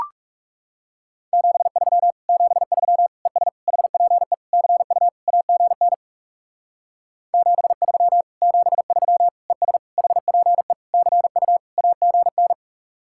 Morse code, heard as a keyed tone, about 38 wpm, 700 Hz. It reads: 73 73 ES HPE CU AGN 73 73 ES HPE CU AGN
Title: Speed-Racing Common QSO Phrases Encore - 30wpm